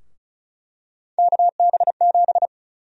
Morse code, heard as a keyed tone, about 34 wpm, 700 Hz. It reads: KB7
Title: USA Call Sign Prefixes - 35wpm